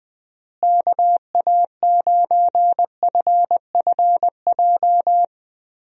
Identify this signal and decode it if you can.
Morse code, heard as a keyed tone, about 21 wpm, 700 Hz. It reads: KA9FFJ
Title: USA Call Signs - 6 Characters - Repeated 3x - 20wpm